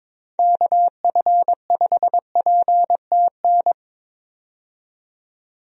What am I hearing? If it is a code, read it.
Morse code, heard as a keyed tone, about 22 wpm, 700 Hz. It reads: KF5PTN
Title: USA Call Signs - 6 Characters - 22wpm